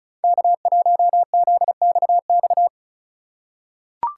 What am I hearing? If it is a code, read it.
Morse code, heard as a keyed tone, about 34 wpm, 700 Hz. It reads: K1ZXX
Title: USA Call Signs - 4x Character Spacing - 35wpm